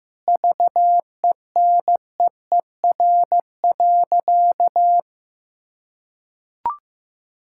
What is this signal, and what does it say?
Morse code, heard as a keyed tone, about 15 wpm, 700 Hz. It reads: VENEER.
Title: Pangram Challenge v2 - 15wpm